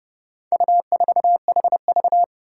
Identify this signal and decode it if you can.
Morse code, heard as a keyed tone, about 30 wpm, 700 Hz. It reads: U4HV
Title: Four Challenging Letter-Number Combinations 30wpm